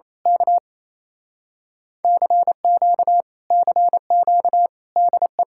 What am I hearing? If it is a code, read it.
Morse code, heard as a keyed tone, about 26 wpm, 700 Hz. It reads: K CQ CQ DE
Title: Speed-Racing Common QSO Phrases - 28wpm